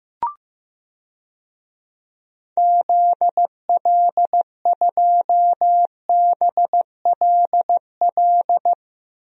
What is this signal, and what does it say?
Morse code, heard as a keyed tone, about 15 wpm, 700 Hz. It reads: ZL2BLL
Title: DXCC-100 Call Signs - 6 Characters - 15wpm